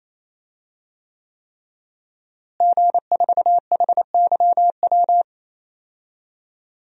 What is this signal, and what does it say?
Morse code, heard as a keyed tone, about 28 wpm, 700 Hz. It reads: G4HYW
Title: DXCC-100 Call Signs - 5 Characters - Repeated 3x - 28wpm